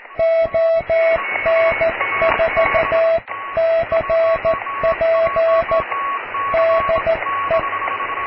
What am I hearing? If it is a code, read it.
Morse code, heard as a keyed tone, about 14 wpm, 660 Hz, with radio noise. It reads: ON4CP DE
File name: cw / QSO+QRM